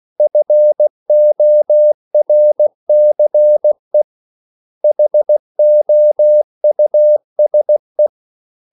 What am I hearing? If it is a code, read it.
Morse code, heard as a keyed tone, about 16 wpm, 600 Hz. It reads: FORCE HOUSE
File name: CW / Practice / 500MostCommonEnglishWords16